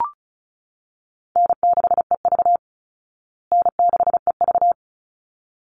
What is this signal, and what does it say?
Morse code, heard as a keyed tone, about 34 wpm, 700 Hz. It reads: N6EV N6EV
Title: Easy Call Signs Repeated 2x - 35wpm